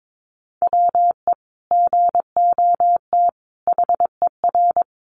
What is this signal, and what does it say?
Morse code, heard as a keyed tone, about 22 wpm, 700 Hz. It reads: WE GOT HER
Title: Sentences from Top 400 Words 22wpm